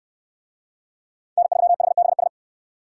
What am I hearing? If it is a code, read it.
Morse code, heard as a keyed tone, about 69 wpm, 700 Hz. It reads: N3HBS